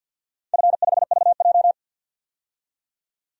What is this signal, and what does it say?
Morse code, heard as a keyed tone, about 52 wpm, 700 Hz. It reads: F5VJ
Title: DXCC-100 Call Signs - 4 Characters - 50wpm